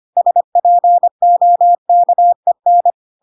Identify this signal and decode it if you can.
Morse code, heard as a keyed tone, about 25 wpm, 700 Hz. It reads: SPOKEN
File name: Pinocchio / Pinocchio ch3 25 25 ik1hge